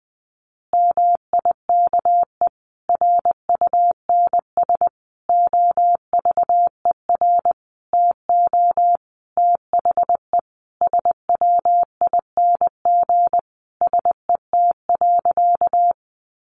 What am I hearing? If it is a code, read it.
Morse code, heard as a keyed tone, about 20 wpm, 700 Hz. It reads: MIKE RUNS OVER TO THE SWING SET.
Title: Additional Easy Phrases 20wpm